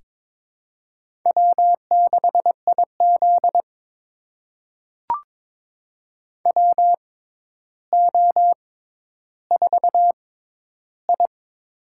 Morse code, heard as a keyed tone, about 23 wpm, spaced as for 11 wpm, 700 Hz. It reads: W6IZ WO4I
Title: USA Call Signs - 4 Characters - 6x Character Spacing - 22wpm